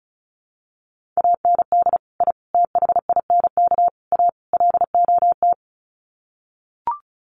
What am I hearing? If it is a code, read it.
Morse code, heard as a keyed tone, about 34 wpm, 700 Hz. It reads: AND I THINK A LOT
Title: Sets of 5 Words Encore2 35wpm